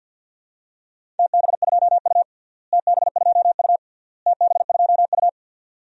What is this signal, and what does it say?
Morse code, heard as a keyed tone, about 50 wpm, 700 Hz. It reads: TB2U TB2U TB2U
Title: DXCC-100 Call Signs - 4 Characters - Repeated 3x - 50wpm